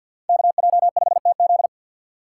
Morse code, heard as a keyed tone, about 51 wpm, 700 Hz. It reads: KJ5TZ
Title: USA Call Signs - 5 Characters - 2x Character Spacing - 50wpm